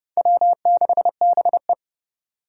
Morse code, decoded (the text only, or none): W6BE